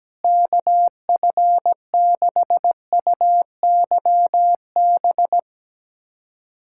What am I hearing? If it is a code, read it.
Morse code, heard as a keyed tone, about 17 wpm, 700 Hz. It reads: KF6UYB
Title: USA Callsign DE Callsign - 6 Characters - 17wpm